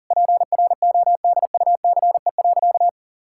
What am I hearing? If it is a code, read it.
Morse code, heard as a keyed tone, about 40 wpm, 700 Hz. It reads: PRODUCE.
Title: Sentences from Top 1000 Words - Review Part 5 - 40wpm